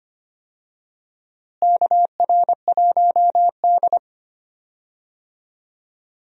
Morse code, heard as a keyed tone, about 25 wpm, 700 Hz. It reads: KR1D
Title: USA Call Signs - 4 Characters - Repeated 3x - 25wpm